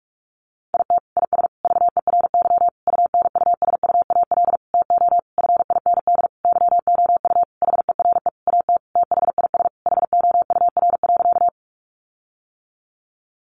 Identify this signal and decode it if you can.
Morse code, heard as a keyed tone, about 44 wpm, 700 Hz. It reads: IT IS VERY UNUSUAL TO FIND YOU HERE AT THIS HOUR.
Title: Sentences from Top 1400 Words 45wpm